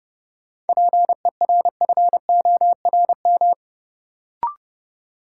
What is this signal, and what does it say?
Morse code, heard as a keyed tone, about 32 wpm, 700 Hz. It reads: PERFORM
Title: Sentences from Top 1000 Words - Review Part 5 - New Word Drill - 30wpm